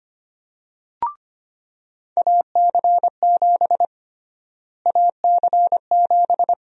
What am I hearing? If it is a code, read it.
Morse code, heard as a keyed tone, about 25 wpm, 700 Hz. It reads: AC7 AC7
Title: USA Call Sign Prefixes - Repeated 2x - 25wpm